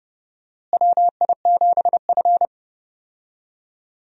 Morse code, heard as a keyed tone, about 30 wpm, 700 Hz. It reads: WI7F
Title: USA Call Signs - 4 Characters - Repeated 3x - 30wpm